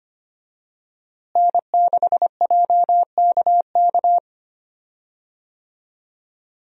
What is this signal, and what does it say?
Morse code, heard as a keyed tone, about 25 wpm, 700 Hz. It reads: N6JKK
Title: USA Call Signs - 5 Characters - Repeated 3x - 25wpm